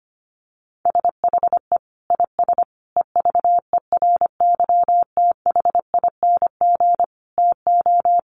Morse code, heard as a keyed tone, about 25 wpm, 700 Hz. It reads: SHE IS EVERYTHING TO
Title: Sentences from Top 700 Words 25wpm